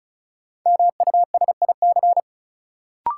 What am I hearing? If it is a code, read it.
Morse code, heard as a keyed tone, about 36 wpm, 700 Hz. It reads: MUSIC
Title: Sentences from Top 300 Words - New Word Drill - 35wpm